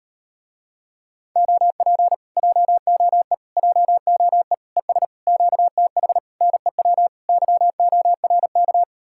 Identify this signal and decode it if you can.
Morse code, heard as a keyed tone, about 38 wpm, 700 Hz. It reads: OP JOE JOE ES QTH NEW YORK